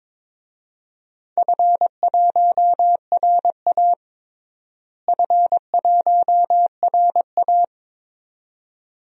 Morse code, heard as a keyed tone, about 22 wpm, 700 Hz. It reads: F1RA F1RA
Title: DXCC-100 Call Signs - 4 Characters - Repeated 2x - 22wpm